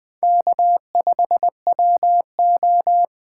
Morse code, heard as a keyed tone, about 20 wpm, 700 Hz. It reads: K5WO
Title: USA Call Signs - 4 Characters - 3x Character Spacing - 20wpm